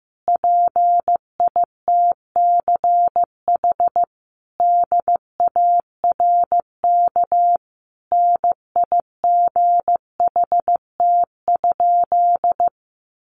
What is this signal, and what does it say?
Morse code, heard as a keyed tone, about 15 wpm, 700 Hz. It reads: PITCH DARK NIGHT?
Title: Sentences from Top 1300 Words 15wpm